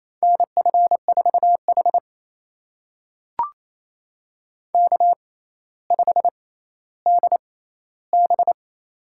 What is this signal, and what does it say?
Morse code, heard as a keyed tone, about 28 wpm, spaced as for 14 wpm, 700 Hz. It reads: NF4H K5DB
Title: USA Call Signs - 4 Characters - 6x Character Spacing - 28wpm